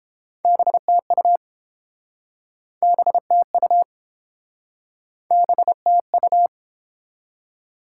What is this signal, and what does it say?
Morse code, heard as a keyed tone, about 30 wpm, 700 Hz. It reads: BTU BTU BTU